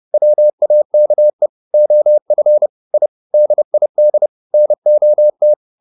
Morse code, heard as a keyed tone, about 30 wpm, 600 Hz. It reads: WAKE OF I DID NOT